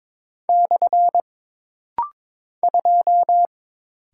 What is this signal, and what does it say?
Morse code, heard as a keyed tone, about 22 wpm, 700 Hz. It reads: / 2
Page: This rapid-fire timing is geared to push your ICR (Instant Character Recognition) skills.